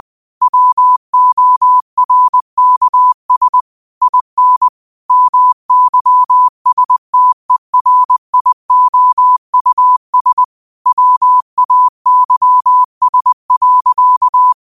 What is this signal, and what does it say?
Morse code, heard as a keyed tone, about 20 wpm, 1000 Hz. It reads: WORKS IN MYSTERIOUS WAYS.
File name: qotdfiles / 20wpm